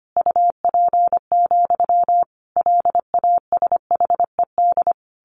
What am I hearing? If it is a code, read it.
Morse code, heard as a keyed tone, about 25 wpm, 700 Hz. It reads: UP, LASHED
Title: A Sound Of Thunder 25wpm